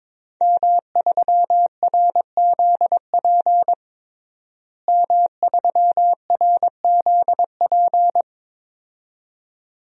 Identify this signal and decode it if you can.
Morse code, heard as a keyed tone, about 22 wpm, 700 Hz. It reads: M3RZP M3RZP
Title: DXCC-100 Call Signs - 5 Characters - Repeated 2x - 22wpm